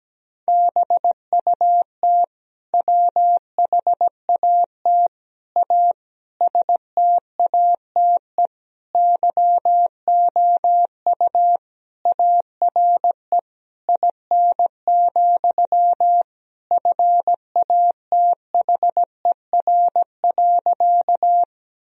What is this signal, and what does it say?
Morse code, heard as a keyed tone, about 17 wpm, 700 Hz. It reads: BUT WHAT A STATE YOU ARE IN, FATHER.
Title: Sentences from Top 1000 Words - Review Part 2 - 17wpm